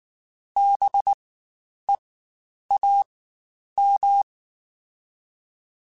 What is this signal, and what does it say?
Morse code, heard as a keyed tone, about 19 wpm, spaced as for 9 wpm, 800 Hz. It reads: BEAM
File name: Spaced Alphas / Spaced Alphas mp3s / Bs sp @18